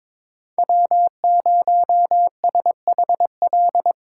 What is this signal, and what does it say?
Morse code, heard as a keyed tone, about 22 wpm, 700 Hz. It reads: W0SHL
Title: K1USN SST QSO Practice - 22wpm